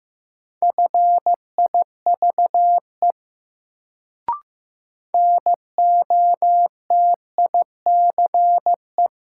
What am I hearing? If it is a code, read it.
Morse code, heard as a keyed tone, about 15 wpm, 700 Hz. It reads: FIVE NOTICE
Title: Sentences from Top 400 Words - New Word Drill - 15wpm